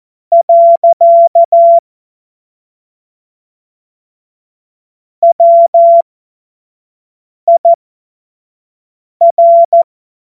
Morse code, heard as a keyed tone, about 14 wpm, spaced as for 5 wpm, 680 Hz. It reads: . WIR
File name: Morsekurs / Witze / Witz 06. September 2022 25 F70